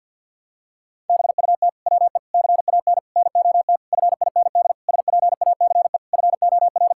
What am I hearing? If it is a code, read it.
Morse code, heard as a keyed tone, about 50 wpm, 700 Hz. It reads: BUT WE CAN NOT FIND SPACE FOR
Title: Sentences from Top 1000 Words - Review Part 5 - 50wpm